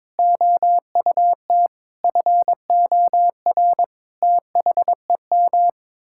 Morse code, heard as a keyed tone, about 21 wpm, 700 Hz. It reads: OUT FOR THEM
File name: practice / 22wpm / Sets of 7 Words - 22wpm